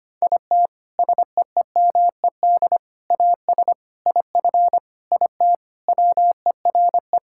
Morse code, heard as a keyed tone, about 25 wpm, 700 Hz. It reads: IT SEEMED AS IF IT WERE
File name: practice / 25wpm / Sentences with 4-Word Phrases - 25wpm